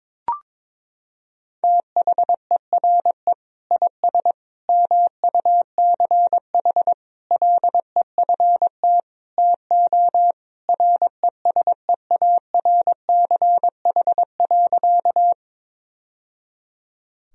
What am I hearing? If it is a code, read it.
Morse code, heard as a keyed tone, about 22 wpm, 700 Hz. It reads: THERE IS MUCH LEFT TO RESEARCH.
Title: Sentences from Top 1000 Words - Review Part 4 - 22wpm